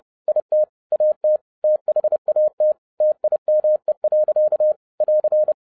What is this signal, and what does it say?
Morse code, heard as a keyed tone, about 30 wpm, 610 Hz. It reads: IT AT THAT TIME. <AR>